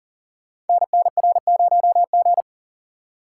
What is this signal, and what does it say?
Morse code, heard as a keyed tone, about 40 wpm, 690 Hz. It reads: NNR0G
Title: Code Groups - 5 Letter:Numbers - 40wpm